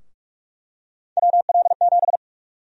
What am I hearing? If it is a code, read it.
Morse code, heard as a keyed tone, about 45 wpm, 700 Hz. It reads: WL7
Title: USA Call Sign Prefixes - Repeated 2x - 45wpm